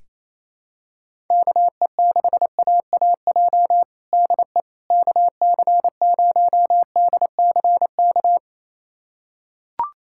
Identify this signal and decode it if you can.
Morse code, heard as a keyed tone, about 27 wpm, 700 Hz. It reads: KE6AAJ DE KC0DCK
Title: USA Callsign DE Callsign - 6 Characters - 28wpm